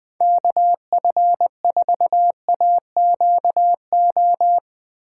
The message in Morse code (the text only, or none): KF4AQO